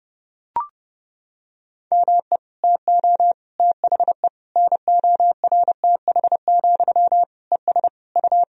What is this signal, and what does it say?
Morse code, heard as a keyed tone, about 30 wpm, 700 Hz. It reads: ME TO THE NORTH, ES U
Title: Sentences from Top 400 Words and Abbreviations 30wpm